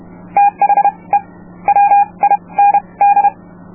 Morse code, heard as a keyed tone, about 31 wpm, spaced as for 24 wpm, 810 Hz. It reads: THE WIND